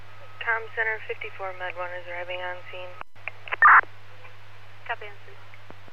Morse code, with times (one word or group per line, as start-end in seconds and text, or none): none